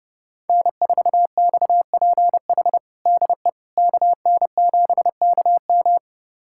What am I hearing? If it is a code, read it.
Morse code, heard as a keyed tone, about 30 wpm, 700 Hz. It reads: N4XPH DE KN7KM
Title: USA Callsign DE Callsign - 5 Characters - 30wpm